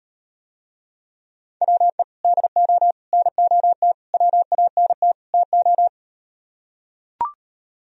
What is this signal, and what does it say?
Morse code, heard as a keyed tone, about 39 wpm, 700 Hz. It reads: WE DO NOT WANT TO